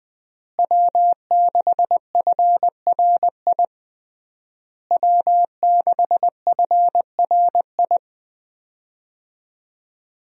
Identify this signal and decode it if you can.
Morse code, heard as a keyed tone, about 20 wpm, 700 Hz. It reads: W6FRI W6FRI
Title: USA Call Signs - 5 Characters - Repeated 3x - 20wpm